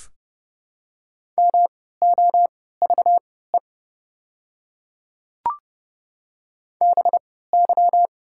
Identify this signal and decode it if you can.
Morse code, heard as a keyed tone, about 30 wpm, spaced as for 17 wpm, 700 Hz. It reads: MOVE BY
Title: 3x Character Spacing - Top 300 Words - 30wpm